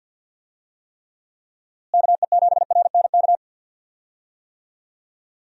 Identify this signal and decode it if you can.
Morse code, heard as a keyed tone, about 50 wpm, 700 Hz. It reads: KE7RNK
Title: USA Call Signs - 6 Characters - Repeated 2x - 50wpm